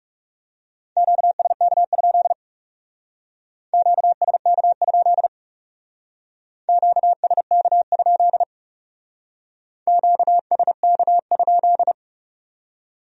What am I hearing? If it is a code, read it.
Morse code, heard as a keyed tone, about 39 wpm, 700 Hz. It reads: QSK? QSK? QSK? QSK?